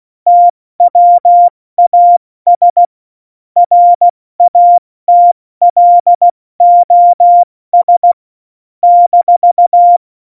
Morse code, heard as a keyed tone, about 16 wpm, 700 Hz. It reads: TWAS RATLOS -
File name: Morsekurs / Witze / Witz 13 March 2018, 70 F80